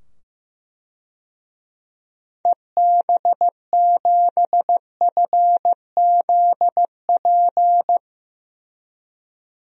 Morse code, heard as a keyed tone, about 16 wpm, 700 Hz. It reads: EB7FZP
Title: DXCC-100 Call Signs - 6 Characters - 15wpm